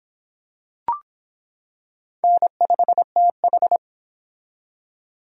Morse code, heard as a keyed tone, about 25 wpm, 700 Hz. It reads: N5TH